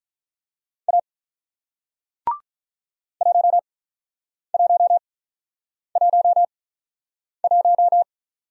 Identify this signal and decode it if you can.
Morse code, heard as a keyed tone, about 48 wpm, spaced as for 9 wpm, 700 Hz. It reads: A 1111